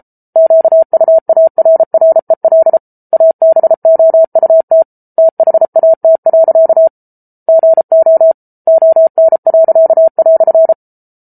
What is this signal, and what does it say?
Morse code, heard as a keyed tone, about 34 wpm, 650 Hz. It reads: QUARREL ABOUT THAT. GO ON."